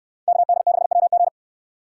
Morse code, heard as a keyed tone, about 69 wpm, 700 Hz. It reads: KD6WG